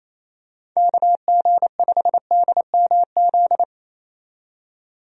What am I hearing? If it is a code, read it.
Morse code, heard as a keyed tone, about 28 wpm, 700 Hz. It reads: KG5DMZ
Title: USA Call Signs - 6 Characters - 28wpm